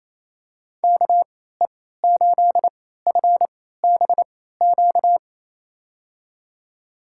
Morse code, heard as a keyed tone, about 28 wpm, spaced as for 16 wpm, 700 Hz. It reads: KE8FBQ
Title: USA Call Signs - 6 Characters - 3x Character Spacing - 28wpm